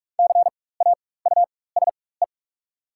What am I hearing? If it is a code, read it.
Morse code, heard as a keyed tone, about 45 wpm, spaced as for 21 wpm, 700 Hz. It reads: CAUSE
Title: 4x Character Spacing - Top 200 Words - 45wpm